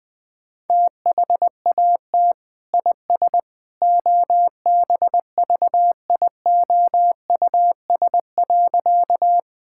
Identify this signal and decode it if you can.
Morse code, heard as a keyed tone, about 20 wpm, 700 Hz. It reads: THAT IS OBVIOUS.
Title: Sentences from Top 1600 Words 20wpm